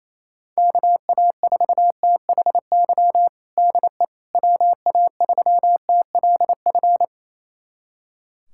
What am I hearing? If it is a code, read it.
Morse code, heard as a keyed tone, about 28 wpm, 700 Hz. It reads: KA4THY DE WA3TLF
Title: USA Callsign DE Callsign 28wpm